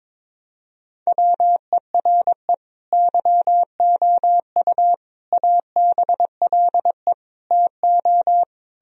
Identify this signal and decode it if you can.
Morse code, heard as a keyed tone, about 22 wpm, 700 Hz. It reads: WERE YOU ABLE TO
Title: Sentences from Top 1000 Words - Review Part 4 - 22wpm